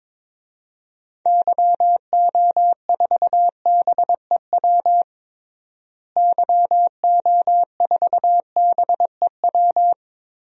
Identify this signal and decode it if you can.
Morse code, heard as a keyed tone, about 22 wpm, 700 Hz. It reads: YO4BEW YO4BEW
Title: DXCC-100 Call Signs - 6 Characters - Repeated 3x - 22wpm